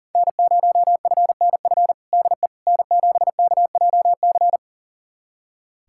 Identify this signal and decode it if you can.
Morse code, heard as a keyed tone, about 40 wpm, 690 Hz. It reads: N0FNF DE N7KJC
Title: USA Callsign DE Callsign - 5 Characters - 40wpm